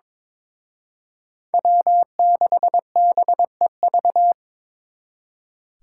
Morse code, heard as a keyed tone, about 26 wpm, 700 Hz. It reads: W6BEV
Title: Speed Racing - USA 1x3 Call Signs 22wpm